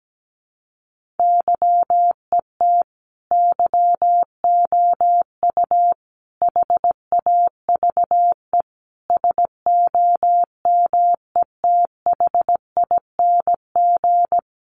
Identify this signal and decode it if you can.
Morse code, heard as a keyed tone, about 17 wpm, 700 Hz. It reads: YET YOU HAVE SOMETHING
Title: Sentences from Top 600 Words 17wpm